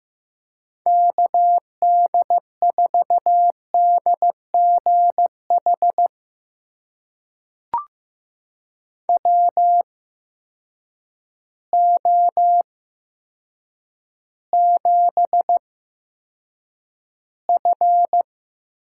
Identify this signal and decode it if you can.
Morse code, heard as a keyed tone, about 15 wpm, spaced as for 7 wpm, 700 Hz. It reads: KD4DGH WO7F
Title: USA Call Signs - 8x Character Spacing - 15wpm